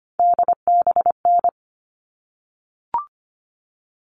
Morse code, heard as a keyed tone, about 25 wpm, 700 Hz. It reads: DBN
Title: All Cut Numbers - 3-Digit 25wpm